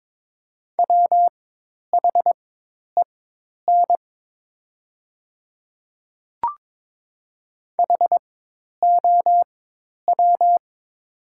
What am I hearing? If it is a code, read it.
Morse code, heard as a keyed tone, about 22 wpm, spaced as for 10 wpm, 700 Hz. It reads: WHEN HOW
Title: 4x Character Spacing - Top 200 Words - 22wpm